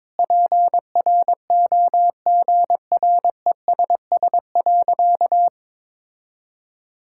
Morse code, heard as a keyed tone, about 22 wpm, 700 Hz. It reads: PROGRESS.
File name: Sentences with 7-Word Phrases - 22wpm